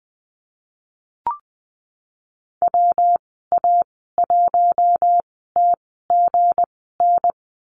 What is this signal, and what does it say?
Morse code, heard as a keyed tone, about 20 wpm, spaced as for 15 wpm, 700 Hz. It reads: WA1TGN
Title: 186 Lesson 30g - Letter b - Call Signs 20wpm